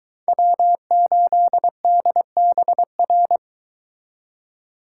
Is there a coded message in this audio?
Yes: Morse code from 0.3 to 3.4 s, heard as a keyed tone, about 23 wpm, 700 Hz. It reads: W8DBR